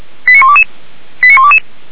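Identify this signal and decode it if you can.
Five-tone selective call (ZVEI), heard as quick runs of five tones, tones of about 80 ms. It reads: 89122 89122
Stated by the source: ZVEI five-tone selective call received on an SDR.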